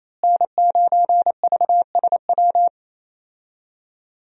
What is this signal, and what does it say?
Morse code, heard as a keyed tone, about 27 wpm, 700 Hz. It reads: N9VSW
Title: USA Call Signs - 5 Characters - Repeated 2x - 28wpm